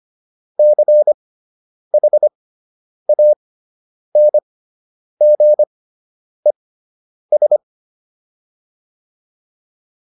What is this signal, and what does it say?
Morse code, heard as a keyed tone, about 25 wpm, spaced as for 9 wpm, 600 Hz. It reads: CHANGES